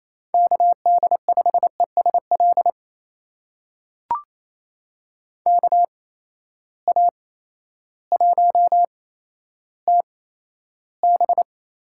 Morse code, heard as a keyed tone, about 28 wpm, spaced as for 13 wpm, 700 Hz. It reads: KD5ESL KA1TB
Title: USA Call Signs - 6 Characters - 8x Character Spacing - 28wpm